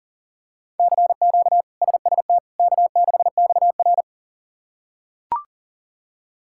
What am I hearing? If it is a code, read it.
Morse code, heard as a keyed tone, about 37 wpm, 700 Hz. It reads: CQ SST K6XR
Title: K1USN SST QSO Practice - 40wpm